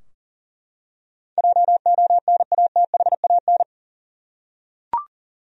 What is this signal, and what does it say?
Morse code, heard as a keyed tone, about 40 wpm, 700 Hz. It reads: JONATHAN
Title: USA Common Names - 40wpm